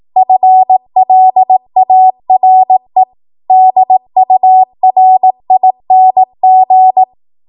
Morse code, heard as a keyed tone, about 18 wpm, 750 Hz. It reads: FLARE DURING
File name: Morse / 18WPM